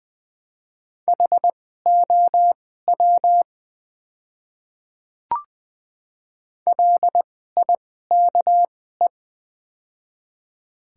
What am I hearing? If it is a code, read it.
Morse code, heard as a keyed tone, about 21 wpm, spaced as for 15 wpm, 700 Hz. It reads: HOW LIKE
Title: 2x Character Spacing - Top 200 Words - 20wpm